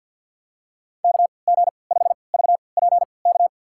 Morse code, heard as a keyed tone, about 53 wpm, spaced as for 29 wpm, 700 Hz. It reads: KG5VPK
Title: USA Call Signs - 6 Characters - 3x Character Spacing - 50wpm